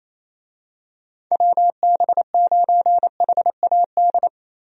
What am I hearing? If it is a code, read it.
Morse code, heard as a keyed tone, about 27 wpm, 700 Hz. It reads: WB9HAD
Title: USA Call Signs - 6 Characters - Repeated 3x - 28wpm